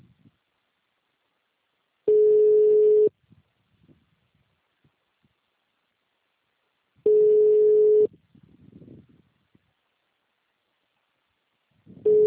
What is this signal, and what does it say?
Ringback tone (Europe, 425 Hz).
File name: out- -2023- - - .5961